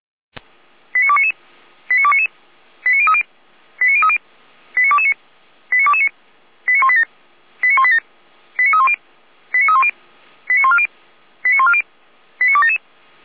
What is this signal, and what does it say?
Five-tone selective call (ZVEI), heard as quick runs of five tones, tones of about 70 ms. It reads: 89299 89299 89030 89030 89229 89229 89187 89187 89310 89310 89140 89140 89277